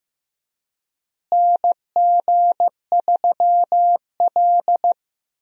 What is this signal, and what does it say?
Morse code, heard as a keyed tone, about 15 wpm, 700 Hz. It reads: NG3L
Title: USA Call Signs - 4 Characters - Repeated 2x - 15wpm